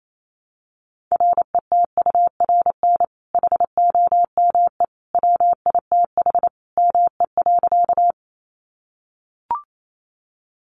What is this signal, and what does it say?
Morse code, heard as a keyed tone, about 28 wpm, 700 Hz. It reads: RETURN HOME WITH ME.
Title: Sentences from Top 800 Words 28wpm